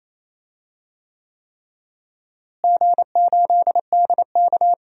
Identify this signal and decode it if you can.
Morse code, heard as a keyed tone, about 28 wpm, 700 Hz. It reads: G8DK
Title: DXCC-100 Call Signs - 4 Characters - Repeated 3x - 28wpm